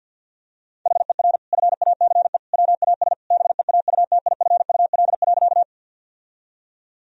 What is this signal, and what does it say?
Morse code, heard as a keyed tone, about 50 wpm, 700 Hz. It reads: HER FACE WAS BEAUTIFUL.
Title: Sentences from Top 1000 Words - Review Part 2 - 50wpm